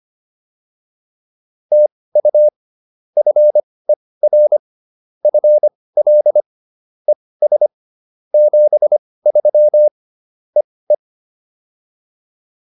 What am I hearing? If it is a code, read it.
Morse code, heard as a keyed tone, about 25 wpm, spaced as for 18 wpm, 600 Hz. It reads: TU FER FL ES 73 EE
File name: POTA203 18